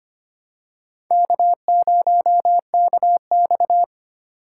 Morse code, heard as a keyed tone, about 25 wpm, 700 Hz. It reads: K0KX
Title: K1USN SST QSO Practice - 25wpm